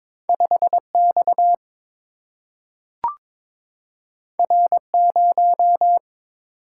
Morse code, heard as a keyed tone, about 23 wpm, 690 Hz. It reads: 5X R0
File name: practice / 22wpm / Two Letter-Number Combinations - 22wpm